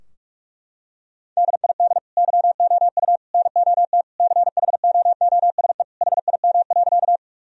Morse code, heard as a keyed tone, about 41 wpm, 700 Hz. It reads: DID YOU NOT CHOOSE HIM.
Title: Sentences from Top 1000 Words - Review Part 4 - 45wpm